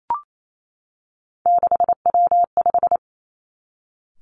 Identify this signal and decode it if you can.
Morse code, heard as a keyed tone, about 28 wpm, 700 Hz. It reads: 6W5
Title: Three Challenging Letter-Number Combinations 28wpm